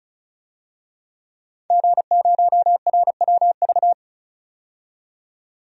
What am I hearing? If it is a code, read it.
Morse code, heard as a keyed tone, about 36 wpm, 700 Hz. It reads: G0RWV
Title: DXCC-100 Call Signs - 5 Characters - 35wpm